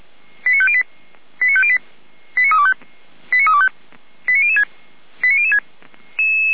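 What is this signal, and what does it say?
Five-tone selective call (ZVEI), heard as quick runs of five tones, tones of about 70 ms. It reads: 89508 89508 89325 89325 89006 89006
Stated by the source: ZVEI five-tone selective call 89011 received by SDR.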